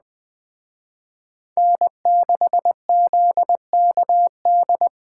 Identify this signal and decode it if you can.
Morse code, heard as a keyed tone, about 21 wpm, 700 Hz. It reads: N6ZKD
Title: Speed Racing - USA 1x3 Call Signs 17wpm